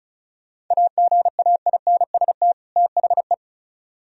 Morse code, heard as a keyed tone, about 43 wpm, 700 Hz. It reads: AGAINST THE
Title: Speed-Racing - Sets of 2 Words 35wpm